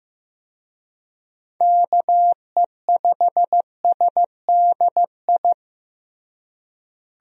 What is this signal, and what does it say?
Morse code, heard as a keyed tone, about 15 wpm, 700 Hz. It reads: KE5SDI
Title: USA Call Signs - 6 Characters - Repeated 2x - 15wpm